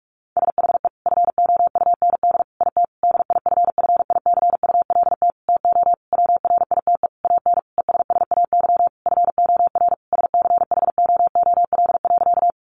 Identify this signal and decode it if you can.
Morse code, heard as a keyed tone, about 44 wpm, 700 Hz. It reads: SHE FOUND IT DIFFICULT TO WRITE AN ESSAY FOR SCHOOL.
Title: Sentences from Top 1500 Words 45wpm